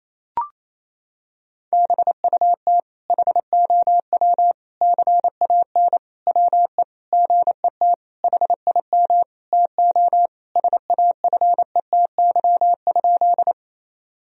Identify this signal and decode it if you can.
Morse code, heard as a keyed tone, about 27 wpm, 700 Hz. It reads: BUT HOW CAN WE GET HIM TO SAFETY?
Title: Sentences from Top 1000 Words - Review Part 4 - 28wpm